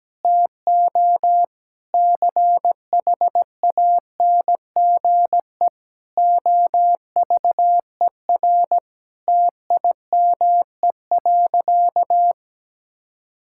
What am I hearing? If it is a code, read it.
Morse code, heard as a keyed tone, about 17 wpm, 700 Hz. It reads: TO CHANGE OVER TIME.
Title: Sentences from Top 1000 Words - Review Part 6 - 17wpm